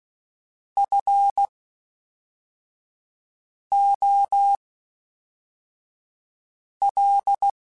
Morse code, heard as a keyed tone, about 16 wpm, 800 Hz. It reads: F O L
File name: Special INGs / Special INGs-mp3s / Fast ing @10